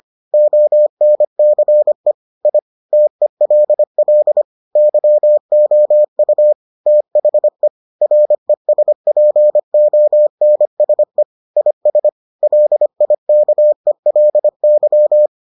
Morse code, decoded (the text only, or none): ONCE I TELL YOU THE RESPONSE IS LIKELY